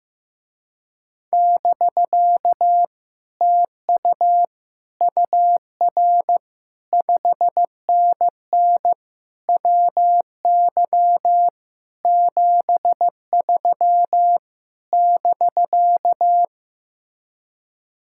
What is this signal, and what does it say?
Morse code, heard as a keyed tone, about 15 wpm, 700 Hz. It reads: <BK> TU UR 5NN WY 73 <BK>
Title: Speed-Racing Common QSO Phrases Encore - 15wpm